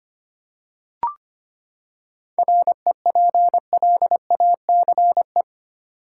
Morse code, heard as a keyed tone, about 25 wpm, 700 Hz. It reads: REPLACE
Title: Seven Letter Words - 25wpm